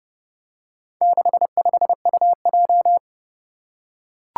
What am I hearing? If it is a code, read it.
Morse code, heard as a keyed tone, about 30 wpm, 700 Hz. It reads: 65UJ
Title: Four Challenging Letter-Number Combinations 30wpm